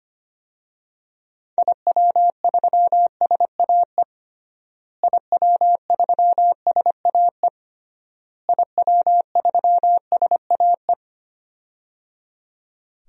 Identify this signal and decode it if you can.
Morse code, heard as a keyed tone, about 25 wpm, 700 Hz. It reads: IW3SAE IW3SAE IW3SAE
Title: DXCC-100 Call Signs - 6 Characters - Repeated 3x - 25wpm